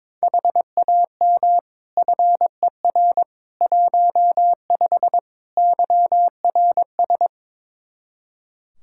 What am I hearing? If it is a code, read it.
Morse code, heard as a keyed tone, about 22 wpm, 700 Hz. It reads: HAM FER 15 YRS
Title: Practice QSO - Exchange 3 - More Info - 22wpm